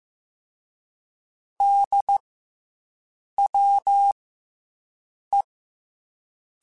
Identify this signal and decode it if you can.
Morse code, heard as a keyed tone, about 15 wpm, 800 Hz. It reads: D W E
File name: Special INGs / Special INGs-mp3s / SlowToFaster ing @13